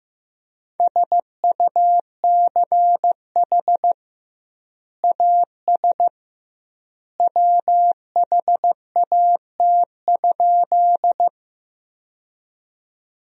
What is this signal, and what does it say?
Morse code, heard as a keyed tone, about 15 wpm, 700 Hz. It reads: SUCH AS WHAT?
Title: 2x Word-Spacing - Sentences from Top 200 Words - 15wpm